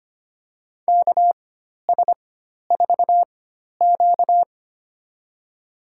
Morse code, heard as a keyed tone, about 25 wpm, spaced as for 12 wpm, 700 Hz. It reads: KS4Q
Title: USA Call Signs - 4 Characters - 4x Character Spacing - 25wpm